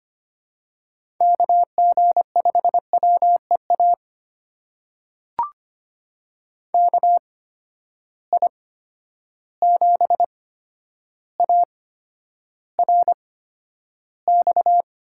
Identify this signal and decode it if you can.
Morse code, heard as a keyed tone, about 25 wpm, spaced as for 11 wpm, 700 Hz. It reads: KG5WEA KI7ARX
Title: USA Call Signs - 6 Characters - 8x Character Spacing - 25wpm